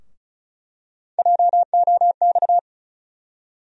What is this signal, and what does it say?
Morse code, heard as a keyed tone, about 36 wpm, 700 Hz. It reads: JOX
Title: USA Call Sign Postfixes - 3 Letters 35wpm